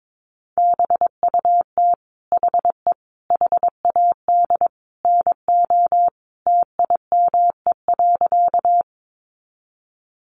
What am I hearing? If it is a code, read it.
Morse code, heard as a keyed tone, about 22 wpm, 700 Hz. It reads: BUT HE HAD NO TIME.
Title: Sentences from Top 100 Words 22wpm